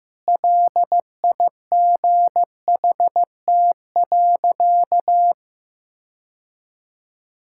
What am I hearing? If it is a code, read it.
Morse code, heard as a keyed tone, about 15 wpm, 700 Hz. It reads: LIGHT.
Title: Sentences from Top 1000 Words - Review Part 6 - 15wpm